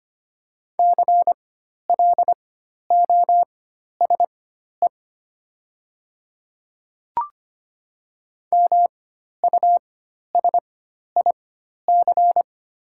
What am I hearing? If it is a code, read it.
Morse code, heard as a keyed tone, about 25 wpm, spaced as for 12 wpm, 700 Hz. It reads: CLOSE MUSIC
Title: 4x Character Spacing - Top 500 Words - 25wpm